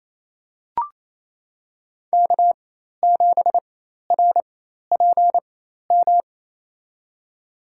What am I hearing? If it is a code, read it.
Morse code, heard as a keyed tone, about 27 wpm, spaced as for 13 wpm, 700 Hz. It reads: K7RPM
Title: USA Call Signs - 5 Characters - 4x Character Spacing - 28wpm